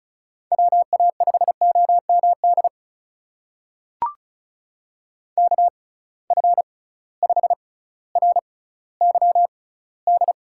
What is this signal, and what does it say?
Morse code, heard as a keyed tone, about 36 wpm, spaced as for 18 wpm, 700 Hz. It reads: WA5OMD KF5RYD
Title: USA Call Signs - 6x Character Spacing - 35wpm